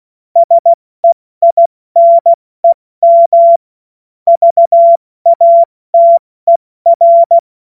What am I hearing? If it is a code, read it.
Morse code, heard as a keyed tone, about 16 wpm, 680 Hz. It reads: SEINEM VATER
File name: Morsekurs / Witze / Witz 07. September 2021, 70 F80